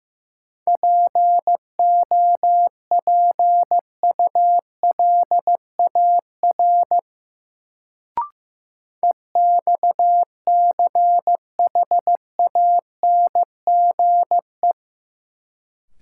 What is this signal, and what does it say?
Morse code, heard as a keyed tone, about 15 wpm, 700 Hz. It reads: POPULAR EXCHANGE
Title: Sets of 5 Words - New Word Drill - 15wpm